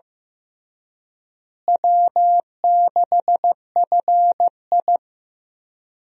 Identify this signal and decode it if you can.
Morse code, heard as a keyed tone, about 15 wpm, 700 Hz. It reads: W6FI
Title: USA Call Signs - 4 Characters - Repeated 3x - 15wpm